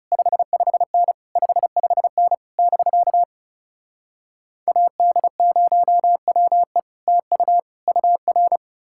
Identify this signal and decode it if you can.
Morse code, heard as a keyed tone, about 37 wpm, 700 Hz. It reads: 55N 55N <BK> AD0WE TU UR
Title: Speed-Racing Common QSO Phrases - 30wpm